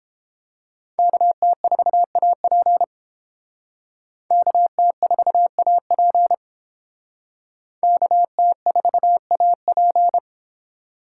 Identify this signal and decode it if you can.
Morse code, heard as a keyed tone, about 30 wpm, 700 Hz. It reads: KT4AP KT4AP KT4AP